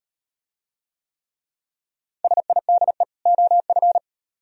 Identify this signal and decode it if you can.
Morse code, heard as a keyed tone, about 38 wpm, 700 Hz. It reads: SIDE OF